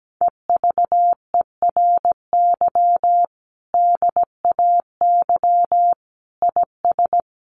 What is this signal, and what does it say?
Morse code, heard as a keyed tone, about 17 wpm, 700 Hz. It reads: EVERY DAY IS
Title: Sentences from Top 1400 Words 17wpm